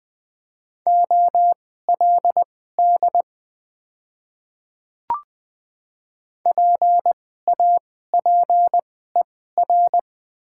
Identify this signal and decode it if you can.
Morse code, heard as a keyed tone, about 20 wpm, spaced as for 14 wpm, 690 Hz. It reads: OLD PAPER
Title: 2x Character Spacing - Top 300 Words - 20wpm